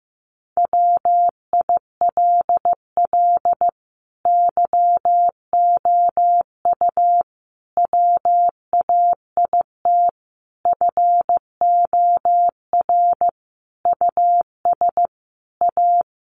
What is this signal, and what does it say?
Morse code, heard as a keyed tone, about 15 wpm, 700 Hz. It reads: WILL YOU WAIT FOR US A
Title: Sentences from Top 900 Words 15wpm